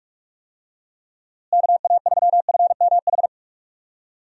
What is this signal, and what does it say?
Morse code, heard as a keyed tone, about 45 wpm, 700 Hz. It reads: KA3FMH
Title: USA Call Signs - 6 Characters - Repeated 2x - 45wpm